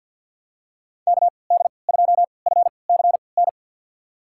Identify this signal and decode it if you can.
Morse code, heard as a keyed tone, about 52 wpm, spaced as for 29 wpm, 700 Hz. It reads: KD2FCN
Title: USA Call Signs - 6 Characters - 3x Character Spacing - 50wpm